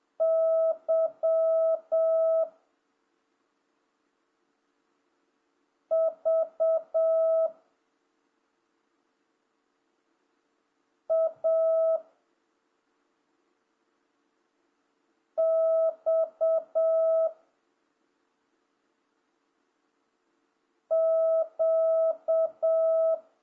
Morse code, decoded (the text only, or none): Y V A X Q